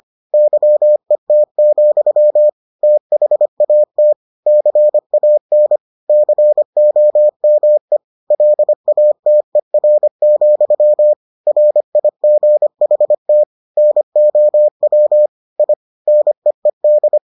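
Morse code, heard as a keyed tone, about 25 wpm, 600 Hz. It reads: YET, THAT CAN COME LATER, RIGHT NOW I NEED
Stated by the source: Morse edition of the Foundations of Amateur Radio episodes.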